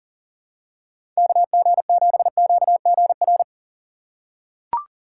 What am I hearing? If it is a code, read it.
Morse code, heard as a keyed tone, about 40 wpm, 700 Hz. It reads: KG7QGR
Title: USA Call Signs - 4x Character Spacing - 40wpm